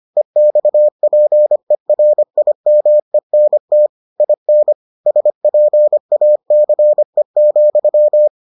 Morse code, heard as a keyed tone, about 25 wpm, 600 Hz. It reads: EXPERIMENT IN SPACE,